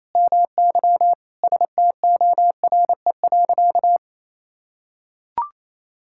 Morse code, heard as a keyed tone, about 28 wpm, 700 Hz. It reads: MY STORE.
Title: Sentences from Top 1000 Words - Review Part 5 - 28wpm